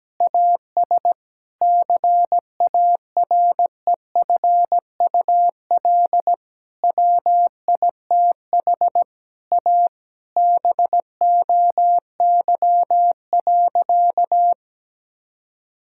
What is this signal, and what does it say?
Morse code, heard as a keyed tone, about 17 wpm, 700 Hz. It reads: AS CAREFUL WITH A BOY.
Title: Sentences from Top 1600 Words 17wpm